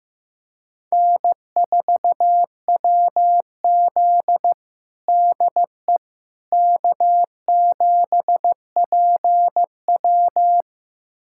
Morse code, heard as a keyed tone, about 15 wpm, 700 Hz. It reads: N4WZ DE K7PW
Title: USA Callsign DE Callsign - 4 Characters - 15wpm